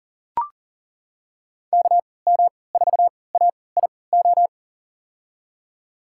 Morse code, heard as a keyed tone, about 40 wpm, spaced as for 23 wpm, 700 Hz. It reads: KM4AIO
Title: USA Call Signs - 6 Characters - 3x Character Spacing - 40wpm